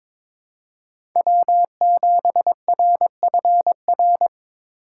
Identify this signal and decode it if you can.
Morse code, heard as a keyed tone, about 22 wpm, 700 Hz. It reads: W7RFR
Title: USA Call Signs - 5 Characters - 4x Character Spacing - 22wpm